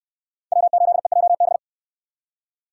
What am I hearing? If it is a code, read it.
Morse code, heard as a keyed tone, about 63 wpm, 700 Hz. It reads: W8EJB